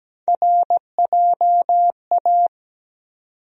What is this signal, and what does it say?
Morse code, heard as a keyed tone, about 17 wpm, 700 Hz. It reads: RJA
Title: USA Call Sign Postfixes - 3 Letters 17wpm